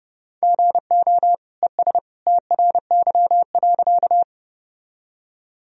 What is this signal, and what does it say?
Morse code, heard as a keyed tone, about 31 wpm, 700 Hz. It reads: GO ES TRY.
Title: Sentences from Top 200 Words and Abbreviations 30wpm